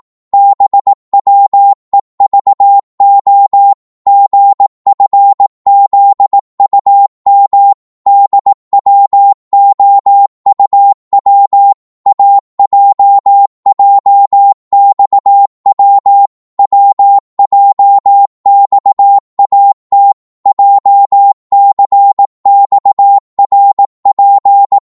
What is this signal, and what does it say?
Morse code, heard as a keyed tone, about 18 wpm, 800 Hz. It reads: BWEVO GFZUM DWOUW AJJXW WJXAT JCXRP